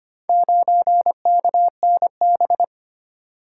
Morse code, heard as a keyed tone, about 25 wpm, 700 Hz. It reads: 9KNB
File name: practice / 25wpm / Four Letter-Number Combinations - 25wpm